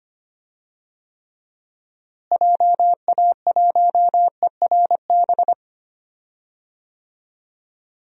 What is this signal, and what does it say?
Morse code, heard as a keyed tone, about 25 wpm, 700 Hz. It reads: JA1ERB
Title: DXCC-100 Call Signs - 6 Characters - Repeated 3x - 25wpm